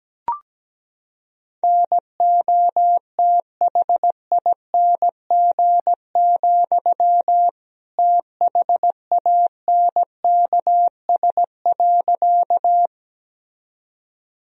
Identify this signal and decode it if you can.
Morse code, heard as a keyed tone, about 17 wpm, 700 Hz. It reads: NOTHING, THANKS.
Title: Sentences from Top 1000 Words - Review Part 3 - 17wpm